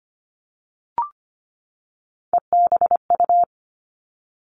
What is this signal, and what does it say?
Morse code, heard as a keyed tone, about 25 wpm, 700 Hz. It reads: EBU
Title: Three Challenging Letter-Number Combinations 25wpm